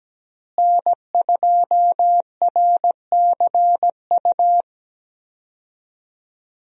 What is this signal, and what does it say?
Morse code, heard as a keyed tone, about 17 wpm, 700 Hz. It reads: N2RCU
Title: USA Call Signs - 5 Characters - Repeated 2x - 17wpm